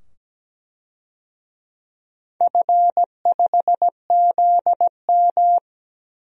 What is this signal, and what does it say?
Morse code, heard as a keyed tone, about 17 wpm, 700 Hz. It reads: F5ZM
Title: DXCC-100 Call Signs - 4 Characters - 17wpm